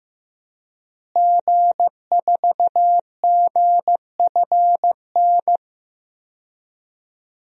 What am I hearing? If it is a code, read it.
Morse code, heard as a keyed tone, about 15 wpm, 700 Hz. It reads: G4GFN
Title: DXCC-100 Call Signs - 5 Characters - Repeated 3x - 15wpm